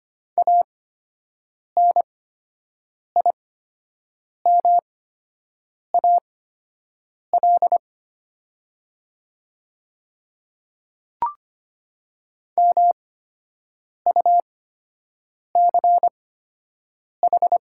Morse code, heard as a keyed tone, about 25 wpm, spaced as for 7 wpm, 700 Hz. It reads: ANIMAL MUCH
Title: 8x Character Spacing - Top 200 Words - 25wpm